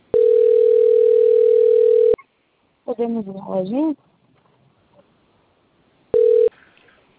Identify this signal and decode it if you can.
Ringback tone (North America), 440 + 480 Hz, with speech.
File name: rg-1600-unknown- - - .2636